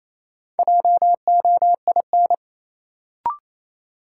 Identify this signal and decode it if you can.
Morse code, heard as a keyed tone, about 28 wpm, 700 Hz. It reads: JOIN
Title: Sentences from Top 1000 Words - Review Part 6 - New Word Drill - 28wpm